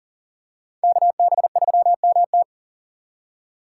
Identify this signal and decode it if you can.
Morse code, heard as a keyed tone, about 40 wpm, 700 Hz. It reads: KB3MT
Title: USA Call Signs - 5 Characters - 3x Character Spacing - 40wpm